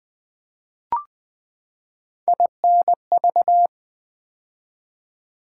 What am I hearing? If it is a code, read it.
Morse code, heard as a keyed tone, about 20 wpm, 700 Hz. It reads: INV
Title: Build A Word - 3 Letters At A Time - 6-Letter Words - 20wpm